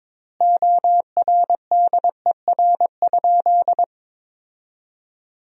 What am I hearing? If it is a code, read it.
Morse code, heard as a keyed tone, about 22 wpm, 700 Hz. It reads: ORDER?
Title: 3x Word-Spacing - Sentences from Top 400 Words - 22wpm